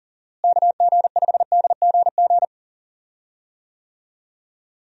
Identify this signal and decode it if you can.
Morse code, heard as a keyed tone, about 40 wpm, 700 Hz. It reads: KG5DGG
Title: USA Call Signs - 6 Characters - Repeated 3x - 40wpm